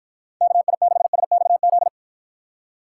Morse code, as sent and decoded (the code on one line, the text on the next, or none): -.- .. -.... ... -..- --..
KI6SXZ